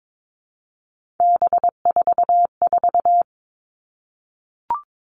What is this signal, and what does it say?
Morse code, heard as a keyed tone, about 22 wpm, 700 Hz. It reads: B44
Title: All Cut Numbers - 3-Digit 22wpm